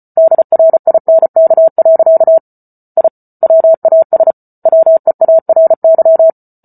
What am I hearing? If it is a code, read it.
Morse code, heard as a keyed tone, about 34 wpm, 650 Hz. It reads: DRINK. I WAS WEARY